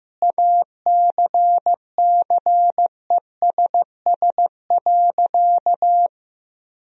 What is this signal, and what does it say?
Morse code, heard as a keyed tone, about 15 wpm, 700 Hz. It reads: ACCESS.
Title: Sentences from Top 1000 Words - Review Part 4 - 15wpm